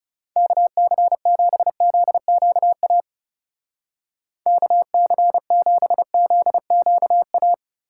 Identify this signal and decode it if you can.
Morse code, heard as a keyed tone, about 36 wpm, 700 Hz. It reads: KC7ZQA KC7ZQA
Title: Speed Racing - USA 2x3 Call Signs 30wpm